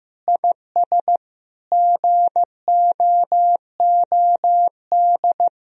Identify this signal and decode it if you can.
Morse code, heard as a keyed tone, about 16 wpm, 700 Hz. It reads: IS GOOD
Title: Sentences from Top 1000 Words - Review Part 4 - 15wpm